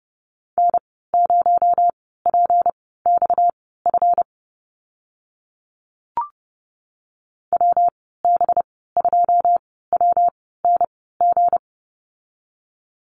Morse code, heard as a keyed tone, about 30 wpm, spaced as for 17 wpm, 700 Hz. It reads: N0PXF WB2WNG
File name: Lesson 37g - Letter x - Call Signs 30wpm